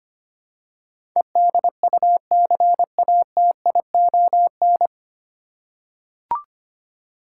Morse code, heard as a keyed tone, about 25 wpm, 690 Hz. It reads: EDUCATION
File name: practice / 25wpm / 9 Letter Words - First 3 Letters Given - 25wpm